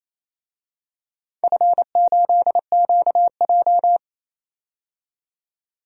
Morse code, heard as a keyed tone, about 28 wpm, 700 Hz. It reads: F8QJ
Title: DXCC-100 Call Signs - 4 Characters - 28wpm